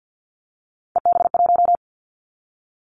Morse code, heard as a keyed tone, about 50 wpm, 700 Hz. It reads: EB1
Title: Callsign Prefixes for Top 25 DXCC - Repeated 3x - 50wpm